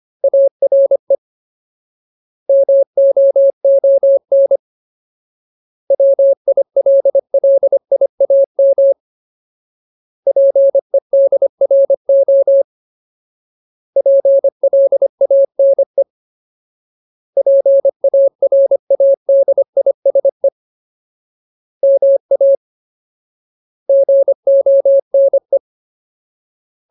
Morse code, heard as a keyed tone, about 25 wpm, 550 Hz. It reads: ARE MOON WILLIAM PEDRO PLANE PARADISE MA GONE
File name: Ultimate Word List / 25WPM Ultimate